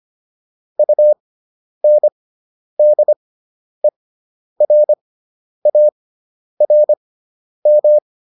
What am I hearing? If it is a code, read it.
Morse code, heard as a keyed tone, about 25 wpm, spaced as for 10 wpm, 600 Hz. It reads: UNDERARM